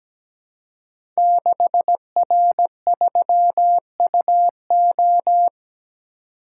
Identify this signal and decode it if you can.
Morse code, heard as a keyed tone, about 17 wpm, 700 Hz. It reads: 6R3UO
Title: Code Groups - 5 Letter:Numbers - 17wpm